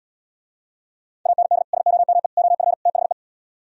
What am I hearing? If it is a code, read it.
Morse code, heard as a keyed tone, about 77 wpm, 700 Hz. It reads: AND SOME OF THE